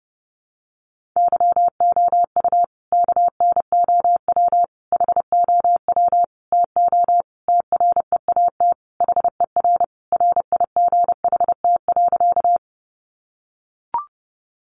Morse code, heard as a keyed tone, about 30 wpm, 700 Hz. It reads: YOU KNOW HOW TO TREAT HER RIGHT.
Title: Sentences from Top 1300 Words 30wpm